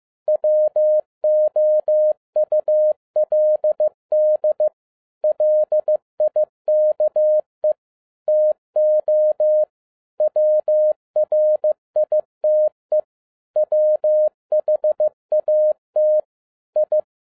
Morse code, heard as a keyed tone, about 15 wpm, 610 Hz. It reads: WOULD LIKE TO WRITE WHAT I
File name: 2011-07-mp3 / n37 THE 3.11 2 15wpm